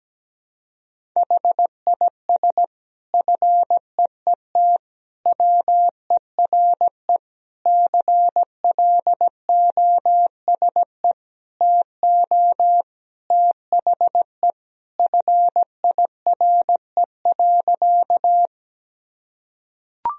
Morse code, heard as a keyed tone, about 17 wpm, 700 Hz. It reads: HIS FEET WERE CLOSE TO THE FIRE.
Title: Sentences from Top 1000 Words - Review Part 1 - 17wpm